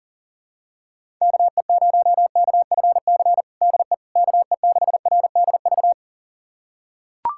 Morse code, heard as a keyed tone, about 40 wpm, 700 Hz. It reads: KE0KFC DE KE6RDV
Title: USA Callsign DE Callsign - 6 Characters - 40wpm